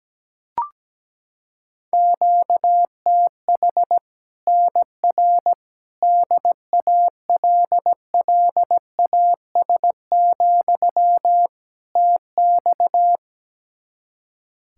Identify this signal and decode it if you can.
Morse code, heard as a keyed tone, about 17 wpm, 700 Hz. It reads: QTH NR DALLAS, TX
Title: Practice QSO - Exchange 1 - Beginning - 17wpm